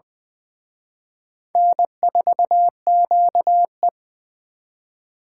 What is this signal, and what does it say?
Morse code, heard as a keyed tone, about 23 wpm, 690 Hz. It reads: N4QE
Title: Speed Racing - USA 1x2 Call Signs 20wpm